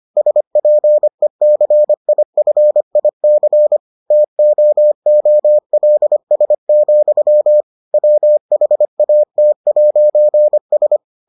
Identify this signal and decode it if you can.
Morse code, heard as a keyed tone, about 25 wpm, 600 Hz. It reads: SPECIFIC TOOLS, WHAT'S